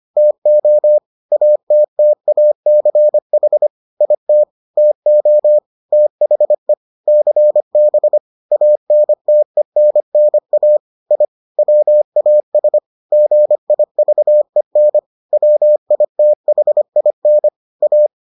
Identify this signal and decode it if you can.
Morse code, heard as a keyed tone, about 25 wpm, 600 Hz. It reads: TO ATTACH IT TO THE CB ANTENNA I WAS GIVEN WITHIN A